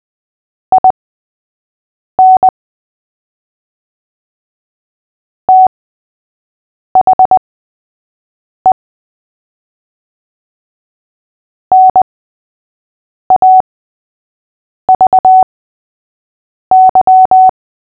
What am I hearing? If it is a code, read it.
Morse code, heard as a keyed tone, about 20 wpm, spaced as for 6 wpm, 750 Hz. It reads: IN THE NAVY
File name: cw / fists / Short-Story-3-6wpm